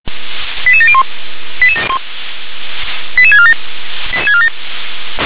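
Five-tone selective call (ZVEI), heard as quick runs of five tones, tones of about 70 ms. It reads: 88701 88647 88647
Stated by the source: ZVEI five-tone selective call received by SDR.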